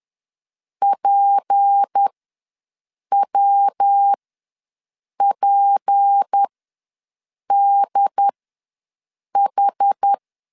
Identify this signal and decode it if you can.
Morse code, heard as a keyed tone, about 10 wpm, spaced as for 6 wpm, 800 Hz. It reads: PWPDH
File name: LEZIONI / 6 PKX0 / 6 PK ws 10wpm 3745